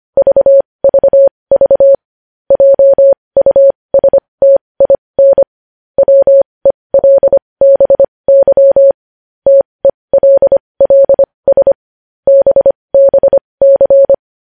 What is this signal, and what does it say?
Morse code, heard as a keyed tone, about 25 wpm, spaced as for 20 wpm, 550 Hz. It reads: VVV JUSTIN WELBY TELLS BBC
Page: news-headlines-in-morse-code-at-20-wpm